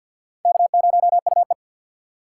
Morse code, heard as a keyed tone, about 52 wpm, 700 Hz. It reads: K0UE
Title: USA Call Signs - 4 Characters - 2x Character Spacing - 50wpm